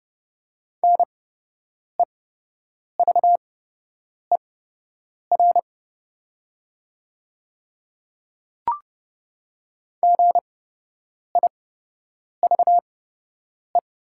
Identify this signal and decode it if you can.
Morse code, heard as a keyed tone, about 30 wpm, spaced as for 8 wpm, 700 Hz. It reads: NEVER GIVE
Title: 8x Character Spacing - Top 300 Words - 30wpm